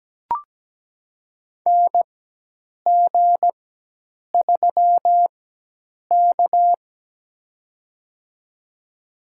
Morse code, heard as a keyed tone, about 17 wpm, spaced as for 8 wpm, 700 Hz. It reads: NG3K
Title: USA Call Signs - 4 Characters - 4x Character Spacing - 17wpm